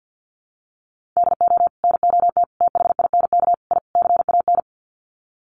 Morse code, heard as a keyed tone, about 50 wpm, 700 Hz. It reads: DO NOT THINK I CAN
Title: Sets of 5 Words Encore2 50wpm